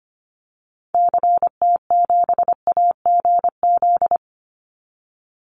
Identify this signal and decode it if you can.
Morse code, heard as a keyed tone, about 25 wpm, 700 Hz. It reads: CT7AGZ
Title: Call Signs - 25wpm